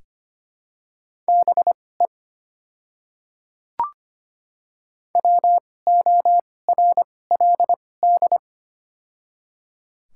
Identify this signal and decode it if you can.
Morse code, heard as a keyed tone, about 25 wpm, spaced as for 18 wpm, 700 Hz. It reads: BE WORLD
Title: 2x Character Spacing - Top 500 Words - 25wpm